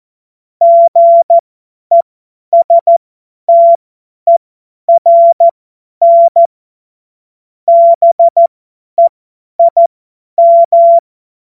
Morse code, heard as a keyed tone, about 14 wpm, spaced as for 10 wpm, 680 Hz. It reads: GESTERN BEIM
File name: Morsekurs / Witze / Witz 21.November 2023 50 F70